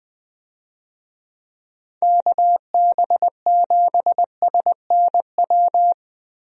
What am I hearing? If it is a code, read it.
Morse code, heard as a keyed tone, about 20 wpm, 700 Hz. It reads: KB7SNW
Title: USA Call Signs - 6 Characters - Repeated 3x - 20wpm